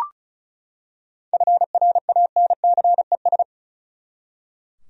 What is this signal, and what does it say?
Morse code, heard as a keyed tone, about 36 wpm, 700 Hz. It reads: FRANCES
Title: USA Common Names - 35wpm